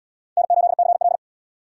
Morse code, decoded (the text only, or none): N8CL